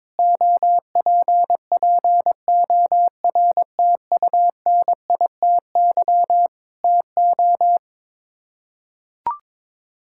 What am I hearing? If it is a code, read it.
Morse code, heard as a keyed tone, about 22 wpm, 700 Hz. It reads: OPPORTUNITY TO
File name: practice / 22wpm / Sets of 8 Words - 22wpm